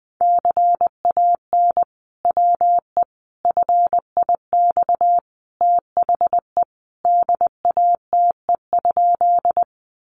Morse code, heard as a keyed tone, about 20 wpm, 700 Hz. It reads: CAN WE FIX THE DATE?
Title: Sentences from Top 1400 Words 20wpm